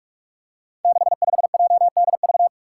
Morse code, heard as a keyed tone, about 46 wpm, 700 Hz. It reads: 65JDV
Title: Five Challenging Letter-Number Combinations 45wpm